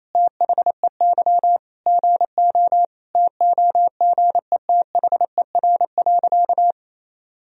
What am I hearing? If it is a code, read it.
Morse code, heard as a keyed tone, about 28 wpm, 700 Hz. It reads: THEY GO TOGETHER.
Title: Sentences from Top 1000 Words - Review Part 3 - 28wpm